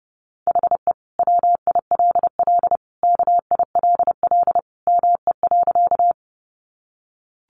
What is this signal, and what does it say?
Morse code, heard as a keyed tone, about 30 wpm, 700 Hz. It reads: HE WILL KILL ME.
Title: Sentences from Top 1400 Words 30wpm